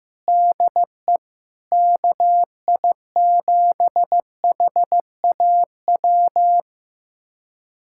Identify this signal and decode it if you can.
Morse code, heard as a keyed tone, about 15 wpm, 700 Hz. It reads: DE KI7HAW
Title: USA Callsign DE Callsign - 6 Characters - 15wpm